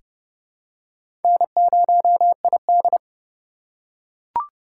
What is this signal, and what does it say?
Morse code, heard as a keyed tone, about 30 wpm, 700 Hz. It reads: N0ID
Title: USA Call Signs - 2x Character Spacing - 30wpm